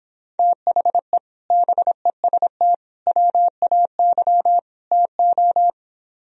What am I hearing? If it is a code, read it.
Morse code, heard as a keyed tone, about 26 wpm, 700 Hz. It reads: THE BEST WAY TO